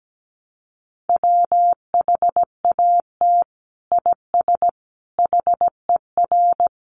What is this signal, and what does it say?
Morse code, heard as a keyed tone, about 17 wpm, 700 Hz. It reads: WHAT IS HER
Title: Sentences from Top 1100 Words 17wpm